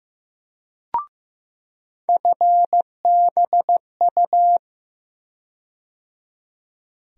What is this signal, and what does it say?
Morse code, heard as a keyed tone, about 15 wpm, 700 Hz. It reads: FBU
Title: USA Call Sign Postfixes - 3 Letters 15wpm